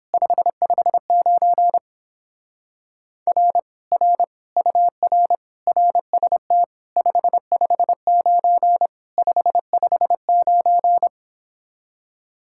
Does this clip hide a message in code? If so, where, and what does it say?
Yes: Morse code from 0.1 to 11.1 s, heard as a keyed tone, about 31 wpm, 700 Hz. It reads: 559 R R UR RST 559 559